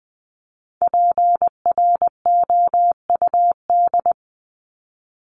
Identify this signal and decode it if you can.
Morse code, heard as a keyed tone, about 20 wpm, 700 Hz. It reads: PROUD
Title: Top 1501-1750 Words 20wpm